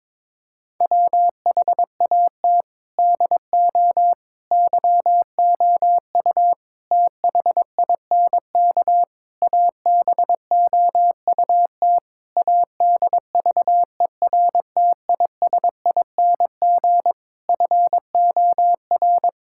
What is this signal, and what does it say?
Morse code, heard as a keyed tone, about 21 wpm, 700 Hz. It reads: WHAT DO YOU THINK ABOUT ADVERTISING FOR
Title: Sentences from Top 1000 Words - Review Part 6 - 22wpm